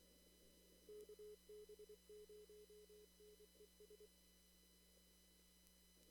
Morse code, heard as a keyed tone, about 25 wpm, 410 Hz. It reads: KB0NES
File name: CW / kt5j wpx cw-op-k5pi